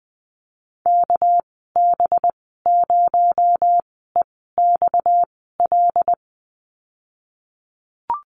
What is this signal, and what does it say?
Morse code, heard as a keyed tone, about 19 wpm, spaced as for 14 wpm, 700 Hz. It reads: KB0EXL
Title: 238 Lesson 38g - Letter q - Call Signs 20wpm